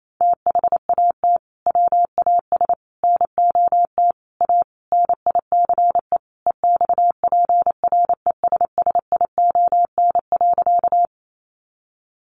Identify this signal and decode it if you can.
Morse code, heard as a keyed tone, about 28 wpm, 700 Hz. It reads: THAT WAS NOT A NICE EXPRESSION.
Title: Sentences from Top 1200 Words 28wpm